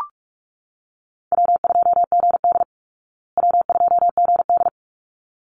Morse code, heard as a keyed tone, about 44 wpm, 700 Hz. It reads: W2GD W2GD
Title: Easy Call Signs Repeated 2x - 45wpm